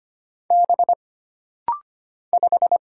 Morse code, heard as a keyed tone, about 25 wpm, 700 Hz. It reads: B 5
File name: Single Character (Letter:Number:Punctuation:Prosign) - Rapid-Fire - 25wpm